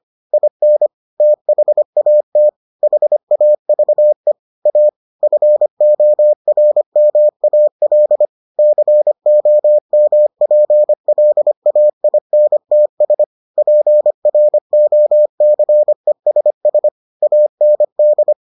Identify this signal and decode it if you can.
Morse code, heard as a keyed tone, about 25 wpm, 600 Hz. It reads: IN THAT HAVE A FORMAL COMPLAINTS PROCESS AND